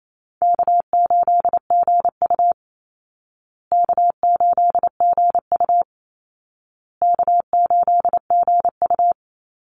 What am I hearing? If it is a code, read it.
Morse code, heard as a keyed tone, about 28 wpm, 700 Hz. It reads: K8GU K8GU K8GU
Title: Easy Call Signs Repeated 3x - 28wpm